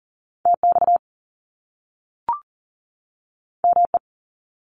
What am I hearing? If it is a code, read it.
Morse code, heard as a keyed tone, about 41 wpm, 700 Hz. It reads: TX ME
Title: US States 40wpm